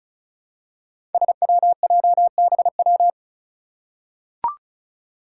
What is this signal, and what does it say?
Morse code, heard as a keyed tone, about 34 wpm, 700 Hz. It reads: SWJBW
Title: Five Challenging Letter-Number Combinations 35wpm